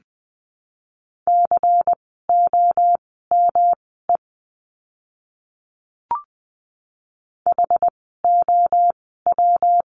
Morse code, heard as a keyed tone, about 20 wpm, spaced as for 15 wpm, 700 Hz. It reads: COME HOW
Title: 237 Lesson 38f - Letter q - Words 20wpm